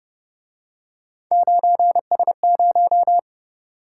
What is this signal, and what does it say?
Morse code, heard as a keyed tone, about 32 wpm, 700 Hz. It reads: 9S0